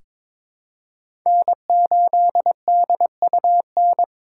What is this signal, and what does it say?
Morse code, heard as a keyed tone, about 23 wpm, 700 Hz. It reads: N8DUN
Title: USA Call Signs - 4x Character Spacing - 22wpm